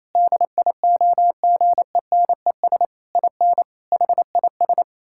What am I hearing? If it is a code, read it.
Morse code, heard as a keyed tone, about 28 wpm, 700 Hz. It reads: DIOGENES IN HIS
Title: Pangram Challenge v2 - 28wpm